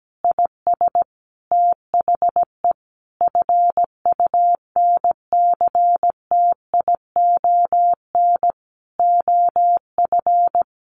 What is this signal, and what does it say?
Morse code, heard as a keyed tone, about 17 wpm, 700 Hz. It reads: IS THE FUNCTION OF
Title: Sentences from Top 1100 Words 17wpm